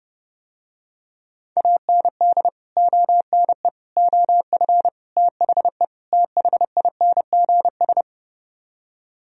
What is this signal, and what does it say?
Morse code, heard as a keyed tone, about 30 wpm, 700 Hz. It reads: AND ONE OF THE THINGS